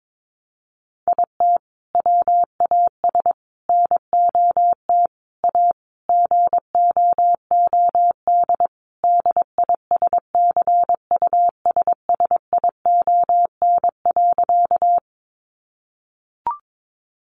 Morse code, heard as a keyed tone, about 22 wpm, 700 Hz. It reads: IT WAS NOT A GOOD DISCUSSION.
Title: Sentences from Top 1100 Words 22wpm